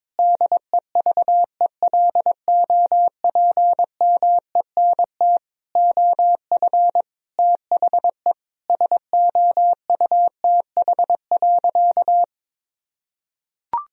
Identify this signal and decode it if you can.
Morse code, heard as a keyed tone, about 22 wpm, 700 Hz. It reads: DEVELOPMENT OF THE SOUTH.
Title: Sentences from Top 1000 Words - Review Part 4 - 22wpm